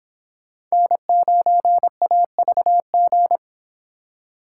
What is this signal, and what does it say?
Morse code, heard as a keyed tone, about 29 wpm, 700 Hz. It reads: N9AVG